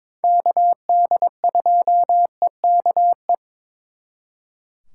Morse code, heard as a keyed tone, about 26 wpm, 700 Hz. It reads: KD2EKE
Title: Speed Racing - USA 2x3 Call Signs 22wpm